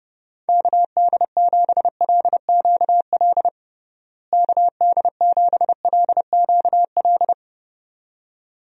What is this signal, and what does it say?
Morse code, heard as a keyed tone, about 32 wpm, 700 Hz. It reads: KD7LQL KD7LQL
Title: USA Call Signs - 6 Characters - Repeated 2x - 30wpm